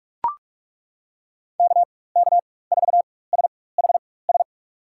Morse code, heard as a keyed tone, about 42 wpm, spaced as for 21 wpm, 700 Hz. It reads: KK4SHS
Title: USA Call Signs - 6 Characters - 4x Character Spacing - 45wpm